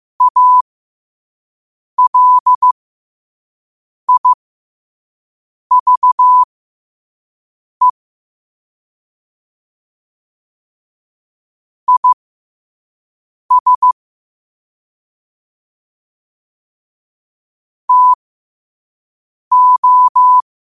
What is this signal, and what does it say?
Morse code, heard as a keyed tone, about 15 wpm, spaced as for 5 wpm, 1000 Hz. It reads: ALIVE IS TO
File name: qotdfiles / 5wpm